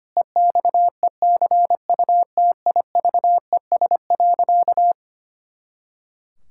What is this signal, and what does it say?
Morse code, heard as a keyed tone, about 25 wpm, 700 Hz. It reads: EXECUTIVES.
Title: Pangram Challenge v2 - 25wpm